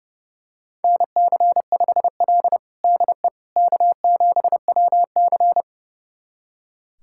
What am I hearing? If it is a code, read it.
Morse code, heard as a keyed tone, about 30 wpm, 700 Hz. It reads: NC5L DE K7WC
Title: USA Callsign DE Callsign - 4 Characters - 30wpm